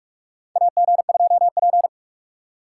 Morse code, heard as a keyed tone, about 46 wpm, 700 Hz. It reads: AG2P
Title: USA Call Signs - 2x Character Spacing - 45wpm